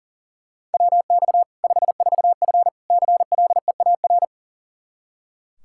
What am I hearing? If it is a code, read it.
Morse code, heard as a keyed tone, about 40 wpm, 700 Hz. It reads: WX 54F CLEAR
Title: K1USN SST QSO Practice 40wpm